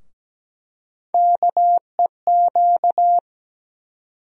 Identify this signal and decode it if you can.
Morse code, heard as a keyed tone, about 17 wpm, 700 Hz. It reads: KEQ
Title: USA Call Sign Postfixes - 3 Letters 17wpm